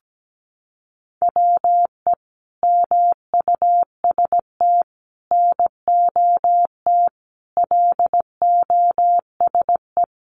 Morse code, heard as a keyed tone, about 17 wpm, 700 Hz. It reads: WE MUST NOT LOSE
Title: Sentences from Top 1100 Words 17wpm